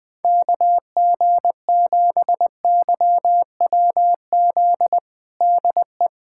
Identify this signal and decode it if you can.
Morse code, heard as a keyed tone, about 20 wpm, 700 Hz. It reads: KG7YWZ DE
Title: USA Callsign DE Callsign - 6 Characters - 20wpm